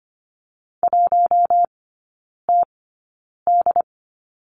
Morse code, heard as a keyed tone, about 25 wpm, spaced as for 9 wpm, 700 Hz. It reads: 1TD
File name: Lesson 15e - Number 5 - Three Characters 25wpm